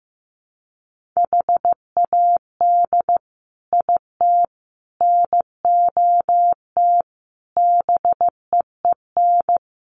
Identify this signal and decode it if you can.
Morse code, heard as a keyed tone, about 15 wpm, 700 Hz. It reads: HAD IT NOT BEEN
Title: Sets of 5 Words Encore2 15wpm